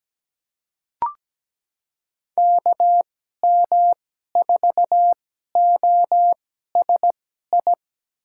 Morse code, heard as a keyed tone, about 17 wpm, spaced as for 12 wpm, 700 Hz. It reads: KM4OSI
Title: USA Call Signs - 6 Characters - 2x Character Spacing - 17wpm